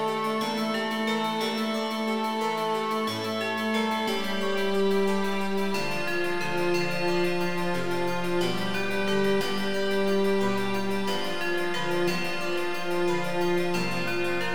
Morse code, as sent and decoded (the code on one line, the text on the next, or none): none